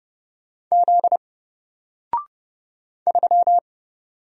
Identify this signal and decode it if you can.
Morse code, heard as a keyed tone, about 30 wpm, 700 Hz. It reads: Z 3
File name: Single-Letter-Number-30wpm